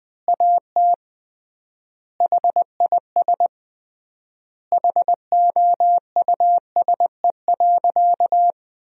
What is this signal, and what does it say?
Morse code, heard as a keyed tone, about 21 wpm, 700 Hz. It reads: AT HIS HOUSE.
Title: 3x Word-Spacing - Sentences from Top 300 Words - 20wpm